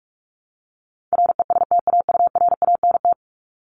Morse code, heard as a keyed tone, about 44 wpm, 700 Hz. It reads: RESTAURANT
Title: Top 1751-2000 Words 45wpm